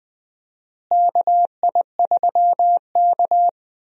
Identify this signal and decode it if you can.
Morse code, heard as a keyed tone, about 20 wpm, 700 Hz. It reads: KI3K
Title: USA Call Signs - 4 Characters - Repeated 3x - 20wpm